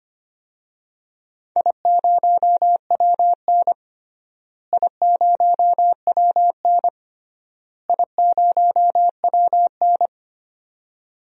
Morse code, heard as a keyed tone, about 25 wpm, 700 Hz. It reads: I0WN I0WN I0WN
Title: DXCC-100 Call Signs - 4 Characters - Repeated 3x - 25wpm